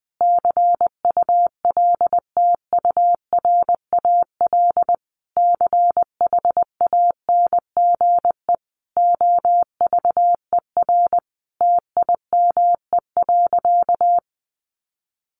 Morse code, heard as a keyed tone, about 20 wpm, 700 Hz. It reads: CULTURAL CHANGE OVER TIME.
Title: Sentences from Top 1500 Words 20wpm